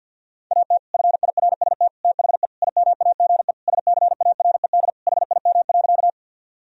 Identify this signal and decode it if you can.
Morse code, heard as a keyed tone, about 50 wpm, 700 Hz. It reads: AT FIRST THE IMAGE SCARED HIM.
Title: Sentences from Top 1000 Words - Review Part 4 - 50wpm